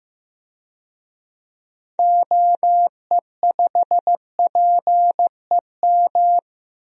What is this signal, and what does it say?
Morse code, heard as a keyed tone, about 15 wpm, 700 Hz. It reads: OE5PEM
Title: DXCC-100 Call Signs - 6 Characters - 15wpm